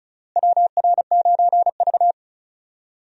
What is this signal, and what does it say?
Morse code, heard as a keyed tone, about 34 wpm, 700 Hz. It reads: WR9V
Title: USA Call Signs - 4 Characters - Repeated 3x - 35wpm